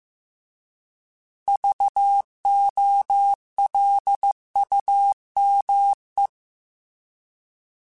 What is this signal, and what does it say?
Morse code, heard as a keyed tone, about 15 wpm, 800 Hz. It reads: VOLUME
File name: Alphabetical / Alphabetical mp3s / Vs @13